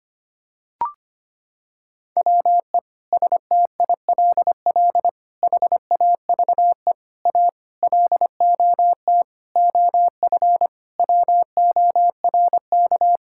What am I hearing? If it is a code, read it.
Morse code, heard as a keyed tone, about 25 wpm, 700 Hz. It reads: WE STILL HAVE A LOT OF WORK
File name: Sets of 9 Words - 25wpm